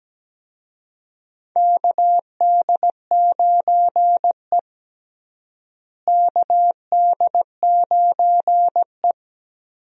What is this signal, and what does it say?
Morse code, heard as a keyed tone, about 17 wpm, 700 Hz. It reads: KD9E KD9E
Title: USA Call Signs - 4 Characters - Repeated 3x - 17wpm